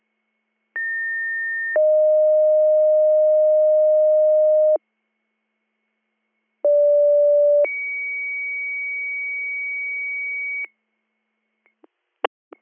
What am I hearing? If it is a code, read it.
Two-tone page, heard as a one-second tone, then a three-second tone, about 1800 then 620 Hz, about 580 then 2250 Hz. It is heard from 0.7 to 10.6 s.